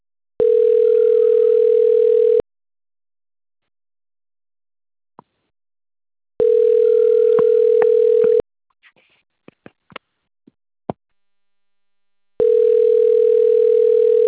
Ringback tone (North America), 440 + 480 Hz.